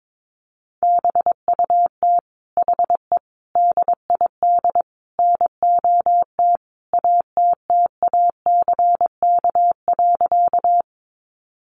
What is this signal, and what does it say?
Morse code, heard as a keyed tone, about 22 wpm, 700 Hz. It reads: BUT HE DID NOT ATTACK.
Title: Sentences from Top 1500 Words 22wpm